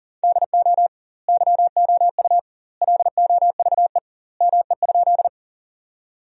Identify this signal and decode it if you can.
Morse code, heard as a keyed tone, about 40 wpm, 700 Hz. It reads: DO YOU LOVE ME?
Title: 2x Word-Spacing - Sentences from Top 400 Words - 40wpm